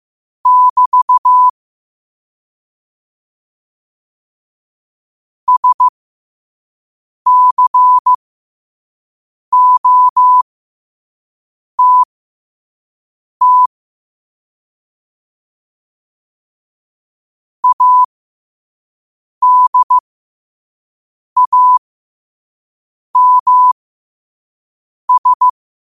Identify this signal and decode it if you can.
Morse code, heard as a keyed tone, about 15 wpm, spaced as for 5 wpm, 1000 Hz. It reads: <BT> SCOTT ADAMS
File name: qotdfiles / 5wpm